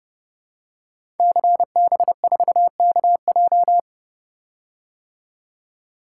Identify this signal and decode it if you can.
Morse code, heard as a keyed tone, about 32 wpm, 700 Hz. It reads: CB4KJ
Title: Code Groups - 5 Letter:Numbers - 30wpm